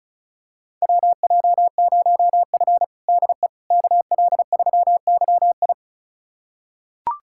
Morse code, heard as a keyed tone, about 36 wpm, 700 Hz. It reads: WJ0F DE KL3YI
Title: USA Callsign DE Callsign 35wpm